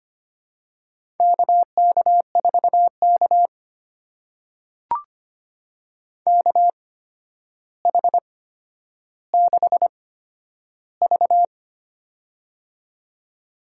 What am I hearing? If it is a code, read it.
Morse code, heard as a keyed tone, about 25 wpm, spaced as for 11 wpm, 700 Hz. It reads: KK4K KH6V
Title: USA Call Signs - 4 Characters - 8x Character Spacing - 25wpm